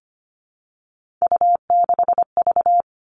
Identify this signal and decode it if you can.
Morse code, heard as a keyed tone, about 25 wpm, 700 Hz. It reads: U6V
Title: All Cut Numbers - 3-Digit 25wpm